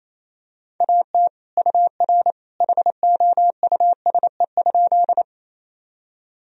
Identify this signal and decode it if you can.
Morse code, heard as a keyed tone, about 28 wpm, 700 Hz. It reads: AT UR HOUSE?
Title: Sentences from Top 200 Words and Abbreviations 28wpm